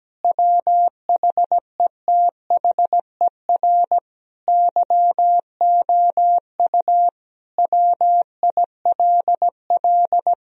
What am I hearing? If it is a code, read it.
Morse code, heard as a keyed tone, about 17 wpm, 700 Hz. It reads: WHETHER YOU WILL
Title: Sentences from Top 1000 Words - Review Part 3 - 17wpm